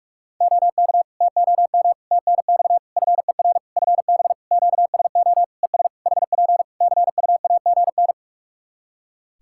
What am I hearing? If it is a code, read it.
Morse code, heard as a keyed tone, about 44 wpm, 700 Hz. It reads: OK TOM TNX FER FB QSO ES HP CUAGN
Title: Practice QSO - Exchange 4 - Ending - 45wpm